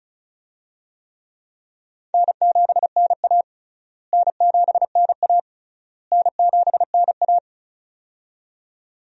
Morse code, heard as a keyed tone, about 34 wpm, 700 Hz. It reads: N7NA N7NA N7NA
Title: USA Call Signs - 4 Characters - Repeated 3x - 35wpm